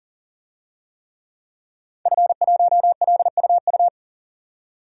Morse code, heard as a keyed tone, about 40 wpm, 700 Hz. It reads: F1LUU
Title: DXCC-100 Call Signs - 5 Characters - 40wpm